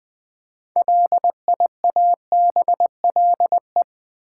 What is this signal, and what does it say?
Morse code, heard as a keyed tone, about 20 wpm, 700 Hz. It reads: LIABLE
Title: Build A Word - 2 Letters At A Time - 6-Letter Words - 20wpm